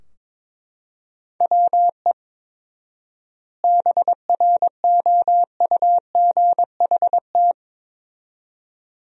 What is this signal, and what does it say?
Morse code, heard as a keyed tone, about 22 wpm, 700 Hz. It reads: WE BROUGHT
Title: 4x Word-Spacing - Sentences from Top 500 Words - 22wpm